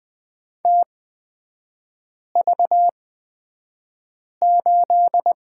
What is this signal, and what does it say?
Morse code, heard as a keyed tone, about 20 wpm, spaced as for 5 wpm, 700 Hz. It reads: TV8
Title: 250 Lesson 40e - BK - Three Characters 20wpm